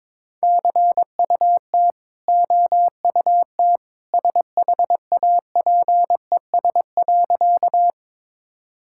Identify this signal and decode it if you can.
Morse code, heard as a keyed tone, about 22 wpm, 690 Hz. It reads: CUT OUT SHAPES.
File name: practice / 22wpm / Easy Sentences with Hints - 22wpm